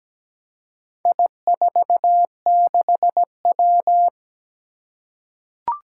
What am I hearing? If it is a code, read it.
Morse code, heard as a keyed tone, about 17 wpm, 700 Hz. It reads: I46W
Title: Four Challenging Letter-Number Combinations 17wpm